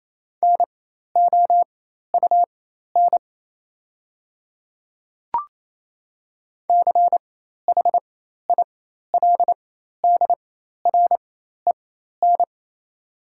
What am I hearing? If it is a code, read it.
Morse code, heard as a keyed tone, about 27 wpm, spaced as for 13 wpm, 700 Hz. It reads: NOUN CHILDREN
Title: 4x Character Spacing - Top 500 Words - 28wpm